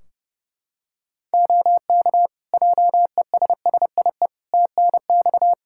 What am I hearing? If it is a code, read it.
Morse code, heard as a keyed tone, about 30 wpm, 700 Hz. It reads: OK JESSIE TNX
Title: Practice QSO - Exchange 4 - Ending - 30wpm